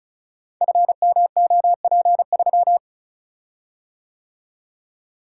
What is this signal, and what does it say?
Morse code, heard as a keyed tone, about 36 wpm, 700 Hz. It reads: FMOP3
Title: Code Groups - 5 Letter:Numbers - 35wpm